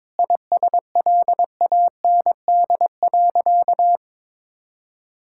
Morse code, heard as a keyed tone, about 22 wpm, 700 Hz. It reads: ISLAND.
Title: Sentences from Top 1000 Words - Review Part 5 - 22wpm